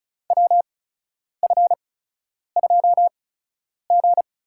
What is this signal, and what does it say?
Morse code, heard as a keyed tone, about 36 wpm, spaced as for 10 wpm, 700 Hz. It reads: WF2G
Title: USA Call Signs - 4 Characters - 8x Character Spacing - 35wpm